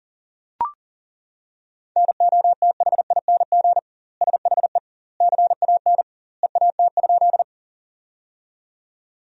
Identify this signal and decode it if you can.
Morse code, heard as a keyed tone, about 40 wpm, 700 Hz. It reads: NOTHING SHE CAN EAT?
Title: 2x Word-Spacing - Sentences from Top 500 Words - 40wpm